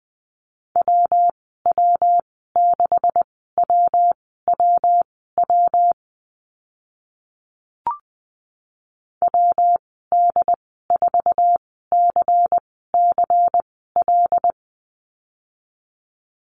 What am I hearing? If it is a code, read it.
Morse code, heard as a keyed tone, about 20 wpm, spaced as for 14 wpm, 700 Hz. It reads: WW6WWW WD4CCL
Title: 140 Lesson 23g - Letter f - Call Signs 20wpm